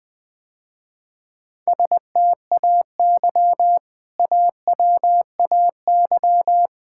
Morse code, heard as a keyed tone, about 21 wpm, 700 Hz. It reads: STAY AWAY
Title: Sentences with 2-Word Phrases - Spoken Beginning - 20wpm